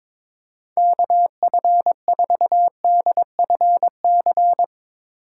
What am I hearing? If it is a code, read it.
Morse code, heard as a keyed tone, about 22 wpm, 700 Hz. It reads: KF4DFC
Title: USA Call Signs - 6 Characters - 6x Character Spacing - 22wpm